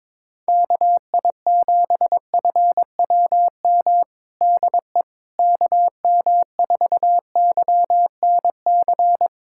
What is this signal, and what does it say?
Morse code, heard as a keyed tone, about 22 wpm, 700 Hz. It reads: KI7FWM DE KM4YNC
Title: USA Callsign DE Callsign - 6 Characters - 22wpm